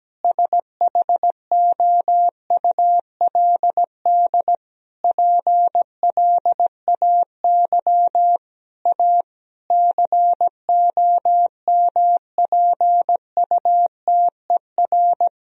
Morse code, heard as a keyed tone, about 17 wpm, 700 Hz. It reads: SHOULD PLAY A COMPUTER
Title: Sentences from Top 1000 Words - Review Part 3 - 17wpm